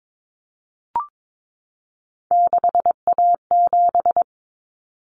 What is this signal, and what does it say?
Morse code, heard as a keyed tone, about 22 wpm, 700 Hz. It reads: 6A7
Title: Common Cut Numbers - 3-Digit 22wpm